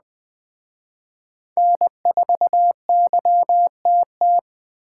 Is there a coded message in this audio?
Yes: Morse code from 1.6 to 4.4 s, heard as a keyed tone, about 19 wpm, 700 Hz. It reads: N4YTT